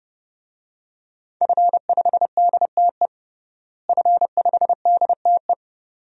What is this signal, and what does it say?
Morse code, heard as a keyed tone, about 28 wpm, 700 Hz. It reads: F5DTE F5DTE
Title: DXCC-100 Call Signs - 5 Characters - Repeated 2x - 30wpm